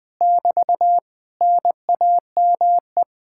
Morse code, heard as a keyed tone, about 21 wpm, 700 Hz. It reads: <BT> NAME
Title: Speed-Racing Common QSO Phrases Encore - 20wpm